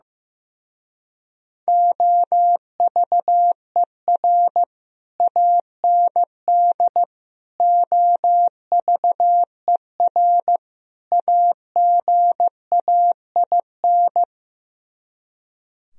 Morse code, heard as a keyed tone, about 16 wpm, 700 Hz. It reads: OVER AND OVER AGAIN
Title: Speed-Racing - Sets of 4 Words 15wpm